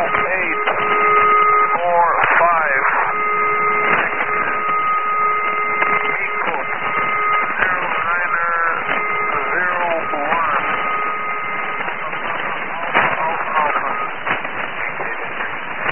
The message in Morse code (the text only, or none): none